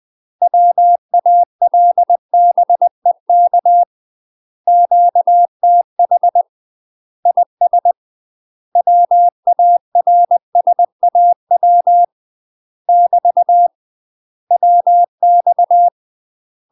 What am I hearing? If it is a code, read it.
Morse code, heard as a keyed tone, about 20 wpm, 700 Hz. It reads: WALBEK QTH IS WARSAW <BT> WX
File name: UK MASTER QSOs TRAINING / 20WPM / UK MASTER QSOs 0040